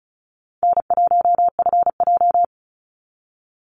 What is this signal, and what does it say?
Morse code, heard as a keyed tone, about 34 wpm, 700 Hz. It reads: N1FJ
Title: NA SOTA Activator Callsigns 35wpm